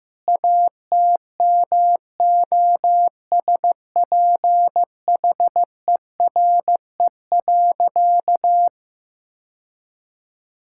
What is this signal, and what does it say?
Morse code, heard as a keyed tone, about 15 wpm, 700 Hz. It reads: ATMOSPHERE.
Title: Sentences from Top 1600 Words 15wpm